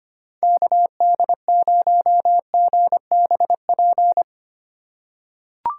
Morse code, heard as a keyed tone, about 25 wpm, 700 Hz. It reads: KD0GBP
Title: USA Call Signs - 6 Characters - 4x Character Spacing - 25wpm